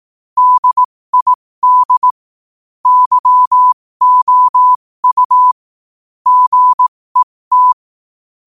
Morse code, heard as a keyed tone, about 18 wpm, spaced as for 15 wpm, 1000 Hz. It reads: DID YOU GET
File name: qotdfiles / 15wpm